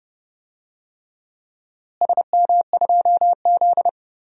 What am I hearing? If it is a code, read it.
Morse code, heard as a keyed tone, about 31 wpm, 700 Hz. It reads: SM2Z
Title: DXCC-100 Call Signs - 4 Characters - Repeated 3x - 30wpm